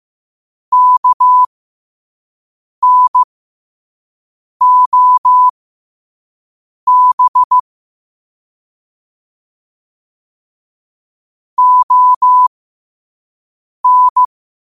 Morse code, heard as a keyed tone, about 15 wpm, spaced as for 5 wpm, 1000 Hz. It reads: KNOB ON
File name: qotdfiles / 5wpm